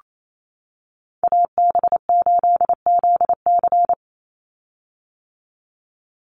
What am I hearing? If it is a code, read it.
Morse code, heard as a keyed tone, about 28 wpm, 700 Hz. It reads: AB8ZC
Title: USA Call Signs 28wpm